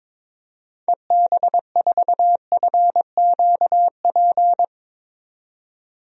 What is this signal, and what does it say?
Morse code, heard as a keyed tone, about 21 wpm, 700 Hz. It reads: EB4FQP
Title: DXCC-100 Call Signs - 6 Characters - 22wpm